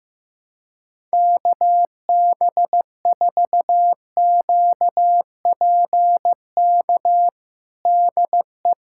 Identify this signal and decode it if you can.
Morse code, heard as a keyed tone, about 16 wpm, 700 Hz. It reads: KB4QPK DE
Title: USA Callsign DE Callsign 15wpm